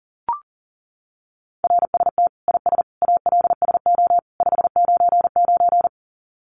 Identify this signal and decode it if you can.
Morse code, heard as a keyed tone, about 40 wpm, 700 Hz. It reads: RST IS ALSO 599
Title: Common QSO Phrases 40wpm